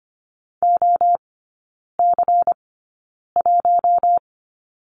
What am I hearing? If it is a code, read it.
Morse code, heard as a keyed tone, about 25 wpm, spaced as for 9 wpm, 700 Hz. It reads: OC1
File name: Lesson 17e - Letter u - Three Characters 25wpm